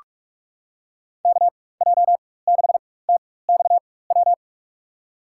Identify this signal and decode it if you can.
Morse code, heard as a keyed tone, about 42 wpm, spaced as for 20 wpm, 700 Hz. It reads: KJ6TXW
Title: USA Call Signs - 6 Characters - 4x Character Spacing - 45wpm